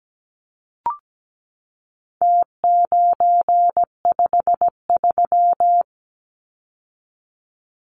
Morse code, heard as a keyed tone, about 17 wpm, 700 Hz. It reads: T953
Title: Numbers - Contest Style - 4-Digit 17wpm